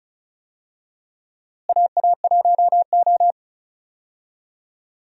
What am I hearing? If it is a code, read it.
Morse code, heard as a keyed tone, about 36 wpm, 700 Hz. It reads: AA1O
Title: USA Call Signs - 4 Characters - Repeated 2x - 35wpm